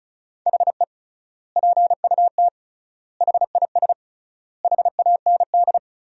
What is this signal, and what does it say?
Morse code, heard as a keyed tone, about 36 wpm, 700 Hz. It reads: HE PUT HIS HAND
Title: 3x Word-Spacing - Sentences from Top 400 Words - 35wpm